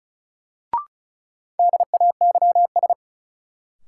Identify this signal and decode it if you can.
Morse code, heard as a keyed tone, about 36 wpm, 700 Hz. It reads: DAYS
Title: Sets of 5 Words Encore - New Word Drill - 35wpm